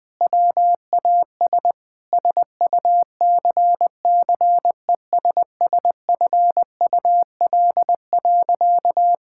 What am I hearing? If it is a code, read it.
Morse code, heard as a keyed tone, about 20 wpm, 700 Hz. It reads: WAS SUCCESSFUL.
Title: Sentences with 7-Word Phrases - Spoken Beginning - 20wpm